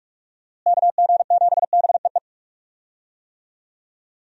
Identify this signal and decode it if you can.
Morse code, heard as a keyed tone, about 42 wpm, 700 Hz. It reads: KG7BEE
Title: USA Call Signs - 6 Characters - Repeated 3x - 45wpm